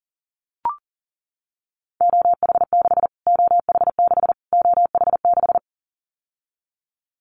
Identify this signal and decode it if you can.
Morse code, heard as a keyed tone, about 37 wpm, 700 Hz. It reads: OH6 OH6 OH6
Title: Callsign Prefixes for Top 25 DXCC - Repeated 3x - 40wpm